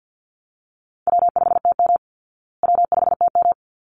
Morse code, heard as a keyed tone, about 49 wpm, 700 Hz. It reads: W5TM W5TM
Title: Call Signs Repeated 3x - 50wpm